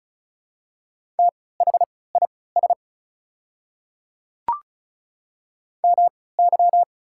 Morse code, heard as a keyed tone, about 36 wpm, spaced as for 20 wpm, 690 Hz. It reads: THIS MY
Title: 3x Character Spacing - Top 200 Words - 35wpm